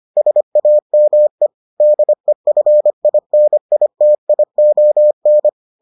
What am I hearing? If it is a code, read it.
Morse code, heard as a keyed tone, about 25 wpm, 600 Hz. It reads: SAME DEFINITION